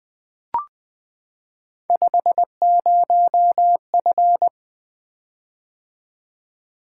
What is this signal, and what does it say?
Morse code, heard as a keyed tone, about 20 wpm, 700 Hz. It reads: 50F
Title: Three Letter-Number Combinations - 20wpm